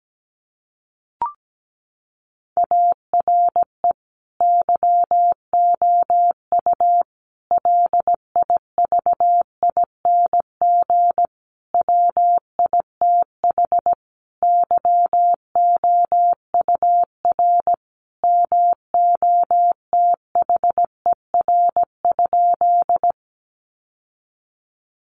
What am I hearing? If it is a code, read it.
Morse code, heard as a keyed tone, about 17 wpm, 700 Hz. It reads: ARE YOU LIVING WITH YOUR MOTHER?
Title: Sentences from Top 1100 Words 17wpm